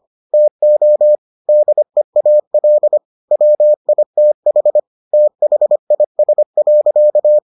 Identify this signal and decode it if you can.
Morse code, heard as a keyed tone, about 25 wpm, 600 Hz. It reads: TO DEAL WITH THIS.